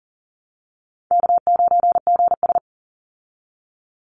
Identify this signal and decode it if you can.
Morse code, heard as a keyed tone, about 40 wpm, 700 Hz. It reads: K9GS
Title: Call Signs Repeated 2x - 40wpm